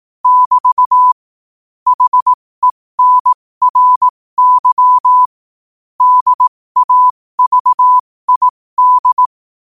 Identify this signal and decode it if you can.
Morse code, heard as a keyed tone, about 18 wpm, spaced as for 15 wpm, 1000 Hz. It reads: <BT> HENRY DAVID
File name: qotdfiles / 15wpm